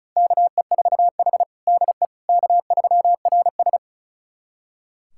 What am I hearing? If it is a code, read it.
Morse code, heard as a keyed tone, about 34 wpm, 700 Hz. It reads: KE4H DE K3RS
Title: USA Callsign DE Callsign - 4 Characters - 35wpm